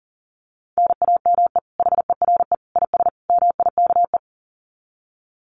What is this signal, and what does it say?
Morse code, heard as a keyed tone, about 40 wpm, 700 Hz. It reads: NAME HERE IS MIKE
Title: Common QSO Phrases 40wpm